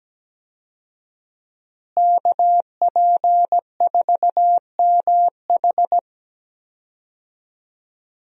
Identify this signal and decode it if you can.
Morse code, heard as a keyed tone, about 17 wpm, 700 Hz. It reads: KP4MH
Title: DXCC-100 Call Signs - 5 Characters - Repeated 2x - 17wpm